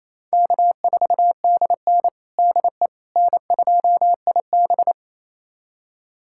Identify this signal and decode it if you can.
Morse code, heard as a keyed tone, about 28 wpm, 700 Hz. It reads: K4DN DE N2IB
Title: USA Callsign DE Callsign - 4 Characters - 28wpm